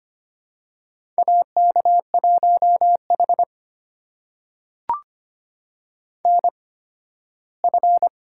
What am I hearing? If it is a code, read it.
Morse code, heard as a keyed tone, about 25 wpm, spaced as for 15 wpm, 700 Hz. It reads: AK1H NF
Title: USA Call Signs - 4 Characters - 8x Character Spacing - 25wpm